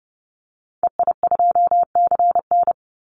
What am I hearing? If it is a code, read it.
Morse code, heard as a keyed tone, about 30 wpm, 700 Hz. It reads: EI2CN
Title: Call Signs Repeated 3x - 30wpm